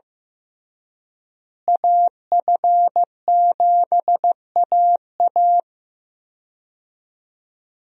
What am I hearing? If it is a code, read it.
Morse code, heard as a keyed tone, about 15 wpm, 700 Hz. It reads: AF7AA
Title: USA Call Signs - 5 Characters - Repeated 3x - 15wpm